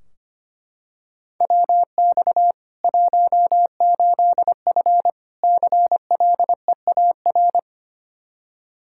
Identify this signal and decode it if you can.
Morse code, heard as a keyed tone, about 25 wpm, 700 Hz. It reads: WX 18F CLEAR
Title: Practice QSO - Exchange 2 - Info - 25wpm